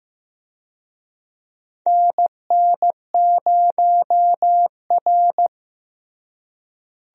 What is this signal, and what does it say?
Morse code, heard as a keyed tone, about 15 wpm, 700 Hz. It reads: NN0R
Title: USA Call Signs - 4 Characters - Repeated 2x - 15wpm